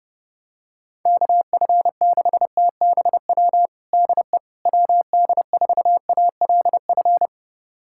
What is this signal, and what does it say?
Morse code, heard as a keyed tone, about 30 wpm, 700 Hz. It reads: KF6TBW DE WD4ALF
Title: USA Callsign DE Callsign - 6 Characters - 30wpm